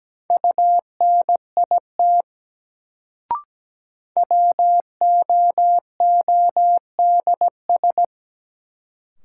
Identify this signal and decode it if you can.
Morse code, heard as a keyed tone, about 17 wpm, 700 Hz. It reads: UNIT WOODS
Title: Sentences from Top 1000 Words - Review Part 2 - New Word Drill - 17wpm